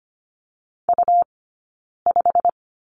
Morse code, heard as a keyed tone, about 25 wpm, 700 Hz. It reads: U 5
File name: Lesson 17d - Letter u - Two Characters 25wpm